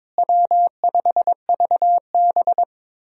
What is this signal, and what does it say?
Morse code, heard as a keyed tone, about 22 wpm, 700 Hz. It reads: W5VB
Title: USA Call Signs - 4 Characters - Repeated 2x - 22wpm